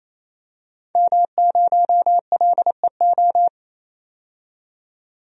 Morse code, heard as a keyed tone, about 28 wpm, 700 Hz. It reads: M0LEO
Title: DXCC-100 Call Signs - 5 Characters - 28wpm